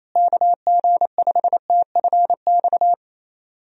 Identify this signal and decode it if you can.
Morse code, heard as a keyed tone, about 28 wpm, 700 Hz. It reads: KG5TFX
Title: USA Call Signs - 6 Characters - Repeated 2x - 28wpm